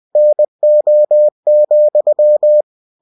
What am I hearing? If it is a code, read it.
Morse code, heard as a keyed tone, about 20 wpm, 600 Hz. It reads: NO,
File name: cw-books / winnie-the-pooh / winnie-20wpm-page-0105